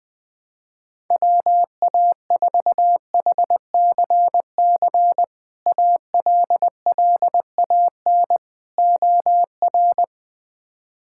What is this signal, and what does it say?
Morse code, heard as a keyed tone, about 20 wpm, 700 Hz. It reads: WA4HCC ALLAN OR
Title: K1USN SST QSO Practice - 20wpm